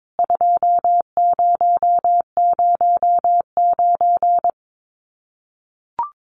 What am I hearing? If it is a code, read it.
Morse code, heard as a keyed tone, about 22 wpm, 700 Hz. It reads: 2009
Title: CWT Exchange 22wpm